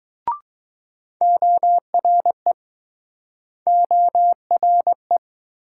Morse code, heard as a keyed tone, about 21 wpm, 700 Hz. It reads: ORE ORE